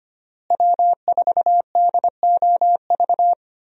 Morse code, heard as a keyed tone, about 25 wpm, 700 Hz. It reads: W4DOV
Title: USA Call Signs - 5 Characters - Repeated 3x - 25wpm